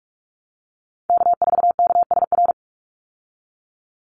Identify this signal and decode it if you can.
Morse code, heard as a keyed tone, about 44 wpm, 700 Hz. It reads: K4KSR
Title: Call Signs Repeated 3x - 45wpm